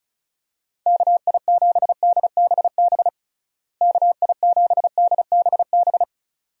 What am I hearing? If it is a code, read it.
Morse code, heard as a keyed tone, about 34 wpm, 700 Hz. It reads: KI7DBB KI7DBB
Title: USA Call Signs - 6 Characters - Repeated 2x - 35wpm